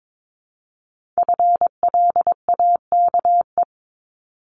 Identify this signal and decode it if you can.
Morse code, heard as a keyed tone, about 22 wpm, 700 Hz. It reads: FLAKE
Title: Multi-letter Phonemes 22wpm